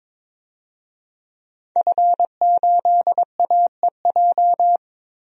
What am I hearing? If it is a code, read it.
Morse code, heard as a keyed tone, about 22 wpm, 700 Hz. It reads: F8AEJ
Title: DXCC-100 Call Signs - 5 Characters - 22wpm